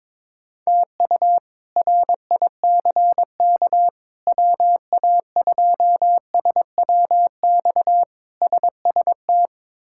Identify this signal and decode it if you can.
Morse code, heard as a keyed tone, about 22 wpm, 700 Hz. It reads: TU RICK WA2SWX SST
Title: K1USN SST QSO Practice - 22wpm